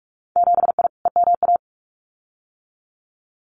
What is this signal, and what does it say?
Morse code, heard as a keyed tone, about 46 wpm, 700 Hz. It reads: 7I EMA
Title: Winter Field Day Exchanges Encore 45wpm